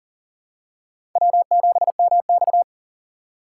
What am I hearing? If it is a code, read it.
Morse code, heard as a keyed tone, about 40 wpm, 700 Hz. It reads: W7MX
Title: USA Call Signs - 4 Characters - 2x Character Spacing - 40wpm